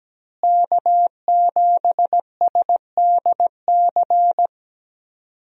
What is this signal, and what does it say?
Morse code, heard as a keyed tone, about 17 wpm, 700 Hz. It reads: K7SDC
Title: USA Call Signs - 5 Characters - Repeated 3x - 17wpm